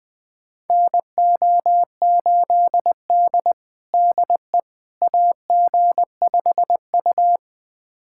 Morse code, heard as a keyed tone, about 20 wpm, 700 Hz. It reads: NO8D DE AG5U
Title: USA Callsign DE Callsign - 4 Characters - 20wpm